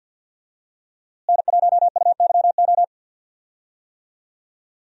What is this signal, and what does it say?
Morse code, heard as a keyed tone, about 48 wpm, 700 Hz. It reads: N1UYO
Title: USA Call Signs - 5 Characters - Repeated 3x - 50wpm